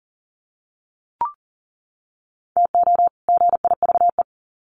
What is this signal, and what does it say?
Morse code, heard as a keyed tone, about 40 wpm, 700 Hz. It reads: TO GIVE
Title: Sets of 2 Words 40wpm